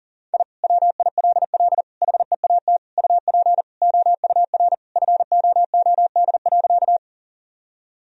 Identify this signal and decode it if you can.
Morse code, heard as a keyed tone, about 40 wpm, 700 Hz. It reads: I WILL HEAT UP OUR FOOD.
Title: Sentences from Top 1000 Words - Review Part 3 - 40wpm